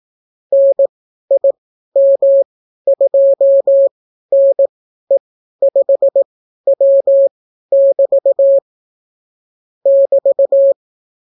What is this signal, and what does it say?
Morse code, heard as a keyed tone, about 18 wpm, spaced as for 12 wpm, 550 Hz. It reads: NIM2NE5W<BT> <BT>